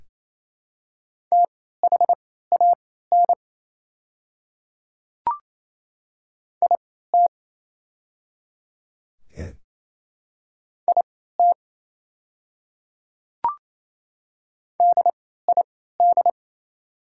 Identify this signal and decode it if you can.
Morse code, heard as a keyed tone, about 28 wpm, spaced as for 16 wpm, 700 Hz. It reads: THAN IT IT DID
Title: 3x Character Spacing - Top 100 Words - 28wpm